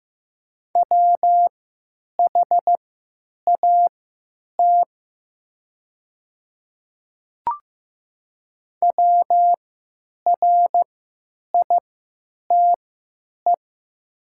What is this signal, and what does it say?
Morse code, heard as a keyed tone, about 15 wpm, spaced as for 9 wpm, 700 Hz. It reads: WHAT WRITE
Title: 3x Character Spacing - Top 100 Words - 15wpm